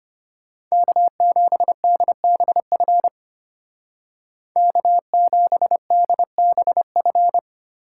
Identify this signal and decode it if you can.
Morse code, heard as a keyed tone, about 33 wpm, 700 Hz. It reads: K7DBF K7DBF